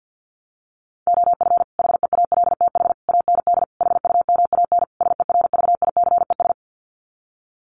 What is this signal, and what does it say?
Morse code, heard as a keyed tone, about 50 wpm, 700 Hz. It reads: OF HEALTH AND HUMAN SERVICES
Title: Sets of 5 Words 50wpm